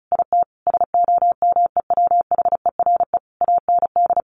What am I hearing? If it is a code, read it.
Morse code, heard as a keyed tone, about 34 wpm, 700 Hz. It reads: IT SOMEWHERE AND
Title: Sentences from Top 1400 Words 35wpm